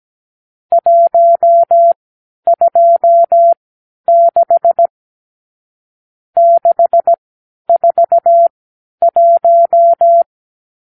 Morse code, heard as a keyed tone, about 17 wpm, spaced as for 10 wpm, 680 Hz. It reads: 126 641
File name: k7qo / Lesson 52 (8)